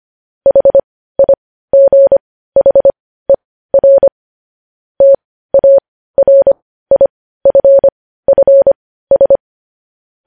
Morse code, heard as a keyed tone, about 25 wpm, spaced as for 15 wpm, 550 Hz. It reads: HIGHER TARIFFS
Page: news-headlines-in-morse-code-at-15-wpm